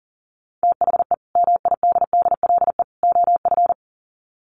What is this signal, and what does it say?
Morse code, heard as a keyed tone, about 40 wpm, 700 Hz. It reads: THE MIDDLE OF
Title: Sets of 3 Words 40wpm